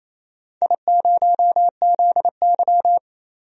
Morse code, heard as a keyed tone, about 28 wpm, 700 Hz. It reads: I0ZY
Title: DXCC-100 Call Signs - 4 Characters - Repeated 2x - 28wpm